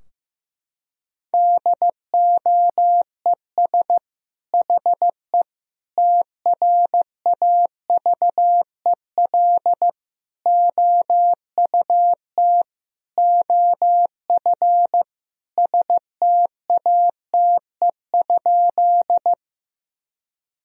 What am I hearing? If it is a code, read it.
Morse code, heard as a keyed tone, about 16 wpm, 700 Hz. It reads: DOES HE TRAVEL OUT OF STATE?
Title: Sentences from Top 1000 Words - Review Part 2 - 15wpm